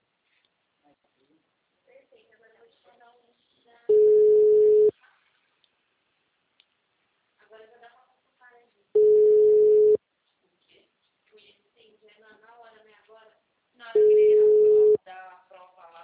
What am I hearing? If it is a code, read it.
Ringback tone (Europe, 425 Hz), with speech.